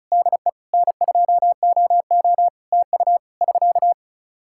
Morse code, heard as a keyed tone, about 34 wpm, 700 Hz. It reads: DE N2OO TU <SK>
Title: K1USN SST QSO Practice 35wpm